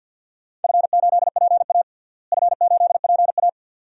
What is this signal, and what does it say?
Morse code, heard as a keyed tone, about 49 wpm, 700 Hz. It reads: F8PA F8PA
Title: DXCC-100 Call Signs - 4 Characters - Repeated 2x - 50wpm